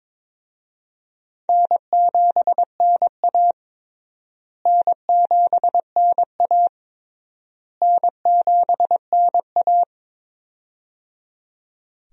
Morse code, heard as a keyed tone, about 21 wpm, 700 Hz. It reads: N7NA N7NA N7NA
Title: USA Call Signs - 4 Characters - Repeated 3x - 22wpm